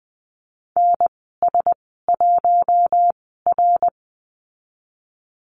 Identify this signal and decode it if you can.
Morse code, heard as a keyed tone, about 20 wpm, spaced as for 15 wpm, 700 Hz. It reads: NS1R
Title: 055 Lesson 10g - Letter r - Call Signs 20wpm